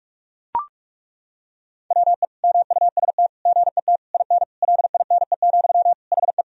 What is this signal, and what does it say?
Morse code, heard as a keyed tone, about 41 wpm, 700 Hz. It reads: WE MUST GET IN LINE, HE
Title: Sentences from Top 1000 Words - Review Part 1 - 45wpm